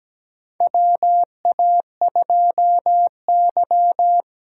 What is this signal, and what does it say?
Morse code, heard as a keyed tone, about 17 wpm, 700 Hz. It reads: WA2Y
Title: USA Call Signs - 6 Characters - 8x Character Spacing - 17wpm